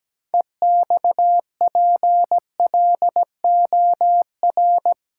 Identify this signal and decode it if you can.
Morse code, heard as a keyed tone, about 17 wpm, 700 Hz. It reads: EXPLOR
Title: Sentences from Top 1100 Words - New Word Drill - 17wpm